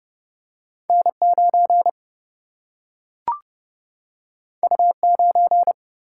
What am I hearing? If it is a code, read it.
Morse code, heard as a keyed tone, about 30 wpm, 700 Hz. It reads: N 9 U 9
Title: Single-Letter Single-Number 30wpm